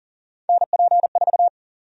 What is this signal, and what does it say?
Morse code, heard as a keyed tone, about 40 wpm, 700 Hz. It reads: NP4
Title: USA Call Sign Prefixes - 40wpm